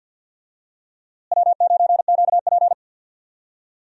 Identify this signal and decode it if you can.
Morse code, heard as a keyed tone, about 51 wpm, 700 Hz. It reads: W9QP
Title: USA Call Signs - 4 Characters - Repeated 3x - 50wpm